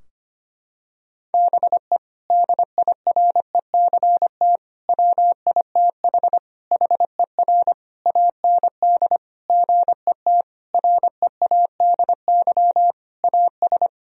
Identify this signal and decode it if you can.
Morse code, heard as a keyed tone, about 25 wpm, 700 Hz. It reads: BE DIRECT WITH HER AND GET READY AS
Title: Sentences from Top 1000 Words - Review Part 2 - 25wpm